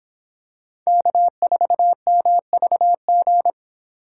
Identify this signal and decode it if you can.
Morse code, heard as a keyed tone, about 26 wpm, 700 Hz. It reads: K4MVG